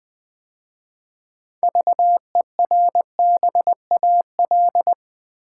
Morse code, heard as a keyed tone, about 20 wpm, 700 Hz. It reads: VERBAL
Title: 6 Letter Words - First 3 Letters Given - 20wpm